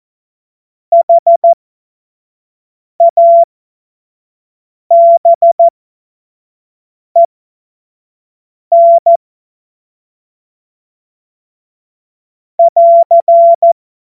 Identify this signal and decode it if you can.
Morse code, heard as a keyed tone, about 14 wpm, spaced as for 5 wpm, 680 Hz. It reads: HABEN <AR>